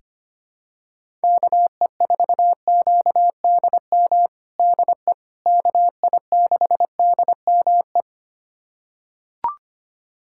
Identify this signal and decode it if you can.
Morse code, heard as a keyed tone, about 25 wpm, 700 Hz. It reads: KE4QDM DE KI6DME
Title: USA Callsign DE Callsign - 6 Characters - 25wpm